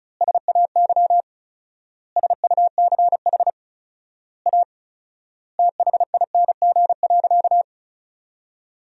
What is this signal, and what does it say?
Morse code, heard as a keyed tone, about 36 wpm, 700 Hz. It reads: SAY SUCH A THING.
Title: 4x Word-Spacing - Sentences from Top 200 Words - 35wpm